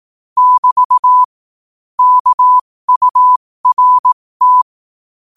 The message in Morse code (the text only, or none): <BT> KURT